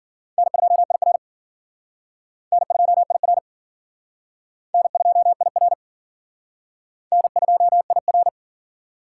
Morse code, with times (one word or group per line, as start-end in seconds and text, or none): 0.4-1.2 N2IR
2.5-3.4 N2IR
4.7-5.7 N2IR
7.1-8.3 N2IR